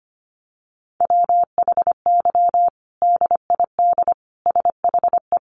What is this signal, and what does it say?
Morse code, heard as a keyed tone, about 25 wpm, 700 Hz. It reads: WHY DID SHE
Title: Sentences from Top 1300 Words 25wpm